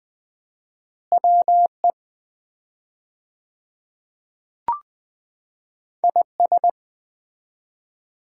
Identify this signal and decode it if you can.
Morse code, heard as a keyed tone, about 20 wpm, 700 Hz. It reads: WE IS
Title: 4x Word-Spacing - Sets of 2 Words - 20wpm